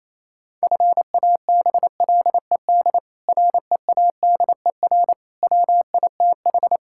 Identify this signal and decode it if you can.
Morse code, heard as a keyed tone, about 27 wpm, 700 Hz. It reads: FABLED READER WITH
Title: Pangram Challenge v2 - 28wpm